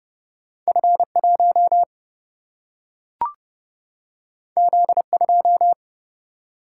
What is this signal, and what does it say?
Morse code, heard as a keyed tone, about 30 wpm, 700 Hz. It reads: F1 Z2
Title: Single-Letter Single-Number 30wpm